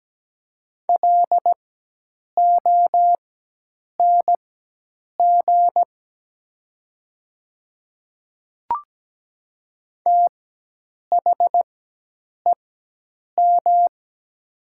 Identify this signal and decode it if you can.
Morse code, heard as a keyed tone, about 17 wpm, spaced as for 8 wpm, 700 Hz. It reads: LONG THEM
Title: 4x Character Spacing - Top 100 Words - 17wpm